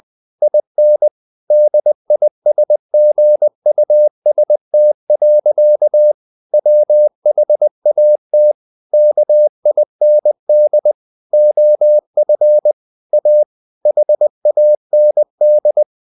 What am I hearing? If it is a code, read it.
Morse code, heard as a keyed tone, about 20 wpm, 600 Hz. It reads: IN DISGUST. WHAT KIND OF A HAND
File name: the old man and the sea / the old man and the sea part 1 20wpm